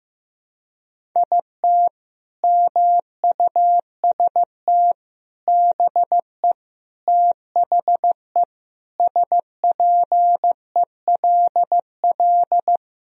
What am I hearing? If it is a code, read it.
Morse code, heard as a keyed tone, about 15 wpm, 700 Hz. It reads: IT MUST BE THE SPELL
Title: Sentences from Top 1000 Words - Review Part 1 - 15wpm